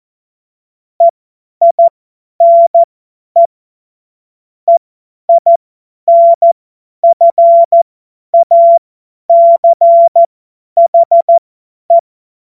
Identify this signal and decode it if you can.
Morse code, heard as a keyed tone, about 14 wpm, spaced as for 10 wpm, 680 Hz. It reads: EINE EINFACHE
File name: Morsekurs / Witze / Witz 12. April 2022 50 F70